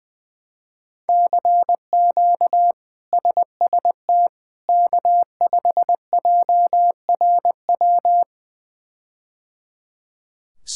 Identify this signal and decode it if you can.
Morse code, heard as a keyed tone, about 20 wpm, 700 Hz. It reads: CQ SST K5JRW
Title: K1USN SST QSO Practice - 20wpm